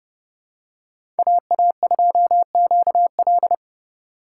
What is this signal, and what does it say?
Morse code, heard as a keyed tone, about 30 wpm, 700 Hz. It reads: AA2QL
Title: USA Call Signs - 5 Characters - 4x Character Spacing - 30wpm